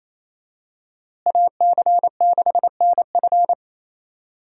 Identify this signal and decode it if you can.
Morse code, heard as a keyed tone, about 28 wpm, 700 Hz. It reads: AC6NF
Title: USA Call Signs - 5 Characters - Repeated 2x - 28wpm